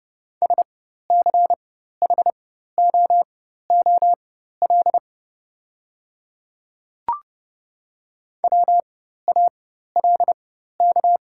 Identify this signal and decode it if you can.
Morse code, heard as a keyed tone, about 29 wpm, spaced as for 14 wpm, 700 Hz. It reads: SCHOOL WALK
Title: 4x Character Spacing - Top 500 Words - 30wpm